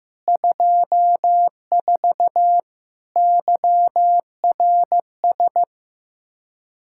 Morse code, heard as a keyed tone, about 15 wpm, 700 Hz. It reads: 24 YRS
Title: K1USN SST QSO Practice 15wpm